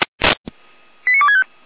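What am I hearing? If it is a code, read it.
Five-tone selective call (ZVEI), heard as quick runs of five tones, tones of about 80 ms. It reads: 89265
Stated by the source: ZVEI five-tone selective call 89240 received by SDR, fire brigade Orlamünde (feuerwehr-orlamuende).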